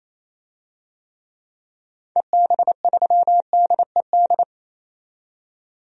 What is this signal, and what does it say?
Morse code, heard as a keyed tone, about 28 wpm, 700 Hz. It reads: EB3DED
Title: DXCC-100 Call Signs - 6 Characters - 28wpm